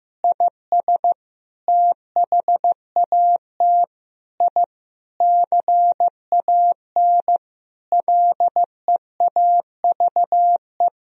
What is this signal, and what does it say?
Morse code, heard as a keyed tone, about 15 wpm, 700 Hz. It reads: IS THAT I CAN LEAVE
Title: Sentences from Top 1000 Words - Review Part 3 - 15wpm